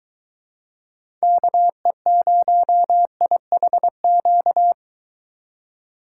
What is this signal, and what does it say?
Morse code, heard as a keyed tone, about 23 wpm, 700 Hz. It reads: KE0IHQ